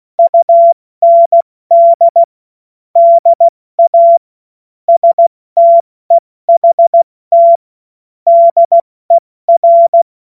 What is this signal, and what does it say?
Morse code, heard as a keyed tone, about 16 wpm, 680 Hz. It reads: UND DA STEHT DER
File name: Morsekurs / Witze / Witz 16. November 2021, 70 F80